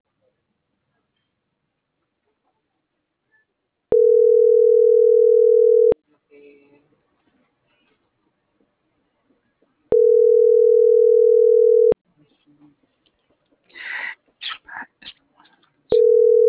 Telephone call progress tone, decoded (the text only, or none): ringback tone